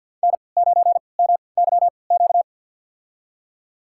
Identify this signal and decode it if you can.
Morse code, heard as a keyed tone, about 48 wpm, spaced as for 28 wpm, 700 Hz. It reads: N9MYQ
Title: USA Call Signs - 5 Characters - 3x Character Spacing - 50wpm